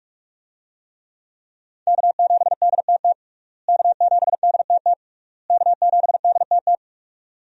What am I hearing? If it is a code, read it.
Morse code, heard as a keyed tone, about 41 wpm, 700 Hz. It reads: K7DTT K7DTT K7DTT
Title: USA Call Signs - 5 Characters - Repeated 3x - 45wpm